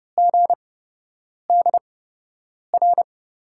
Morse code, heard as a keyed tone, about 30 wpm, spaced as for 8 wpm, 700 Hz. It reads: GDR
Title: USA Call Signs - 8x Character Spacing - 30wpm